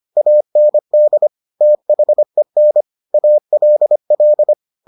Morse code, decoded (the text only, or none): AND THEN ALL